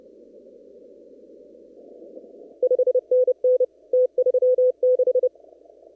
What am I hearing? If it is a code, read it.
Morse code, heard as a keyed tone, about 30 wpm, 500 Hz. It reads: 5NN T36
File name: CW / kt5j wpx cw-op-k5pi